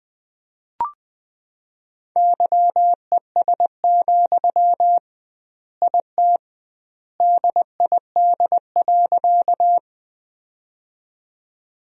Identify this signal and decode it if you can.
Morse code, heard as a keyed tone, about 20 wpm, 700 Hz. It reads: YES, IT DID.
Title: 2x Word-Spacing - Sentences from Top 500 Words - 20wpm